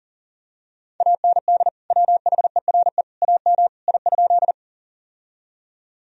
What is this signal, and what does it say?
Morse code, heard as a keyed tone, about 40 wpm, 700 Hz. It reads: AND WHERE AM I?
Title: Sentences from Top 1000 Words - Review Part 2 - 40wpm